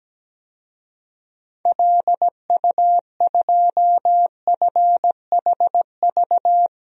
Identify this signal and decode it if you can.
Morse code, heard as a keyed tone, about 17 wpm, 700 Hz. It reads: LU2FHV
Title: DXCC-100 Call Signs - 6 Characters - 17wpm